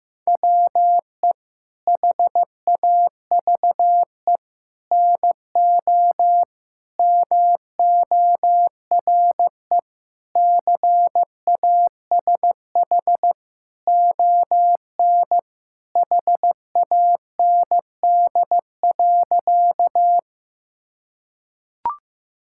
Morse code, heard as a keyed tone, about 15 wpm, 700 Hz. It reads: WE HAVE NO MORE CASH ON HAND.
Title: Sentences from Top 1000 Words - Review Part 5 - 15wpm